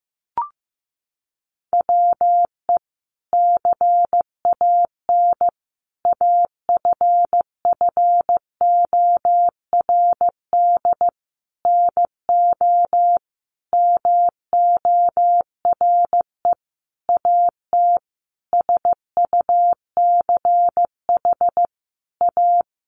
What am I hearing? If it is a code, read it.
Morse code, heard as a keyed tone, about 15 wpm, 700 Hz. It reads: WE CAN AFFORD NO MORE AT SUCH A
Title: Sentences from Top 1100 Words 15wpm